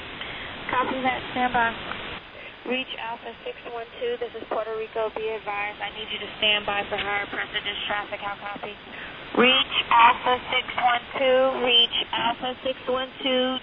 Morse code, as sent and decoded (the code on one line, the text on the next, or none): none